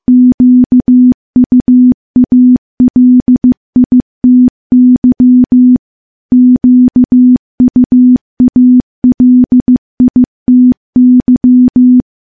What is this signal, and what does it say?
Morse code, heard as a keyed tone, about 15 wpm, 250 Hz. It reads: QUALITY QUALITY
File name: morse / morse-training-15WPM